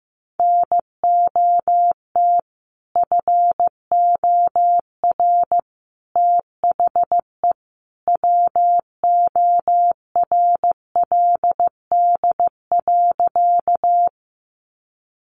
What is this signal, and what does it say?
Morse code, heard as a keyed tone, about 15 wpm, 700 Hz. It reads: NOT FOR THE WORLD.
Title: Sentences from Top 600 Words 15wpm